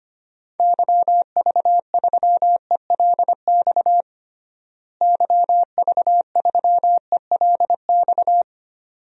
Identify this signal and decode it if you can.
Morse code, heard as a keyed tone, about 25 wpm, 700 Hz. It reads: YV3ELX YV3ELX
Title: DXCC-100 Call Signs - 6 Characters - Repeated 2x - 25wpm